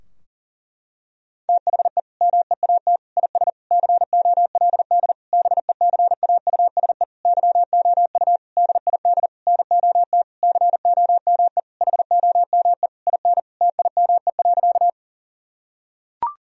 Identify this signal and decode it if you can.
Morse code, heard as a keyed tone, about 37 wpm, 700 Hz. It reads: THE MEAT IS COLD BECAUSE YOU DID NOT COME HOME IN TIME.
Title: Sentences from Top 1000 Words - Review Part 3 - 40wpm